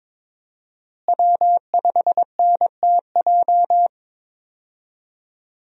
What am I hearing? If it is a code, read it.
Morse code, heard as a keyed tone, about 22 wpm, 700 Hz. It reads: W5NTJ
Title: USA Call Signs - 5 Characters - Repeated 3x - 22wpm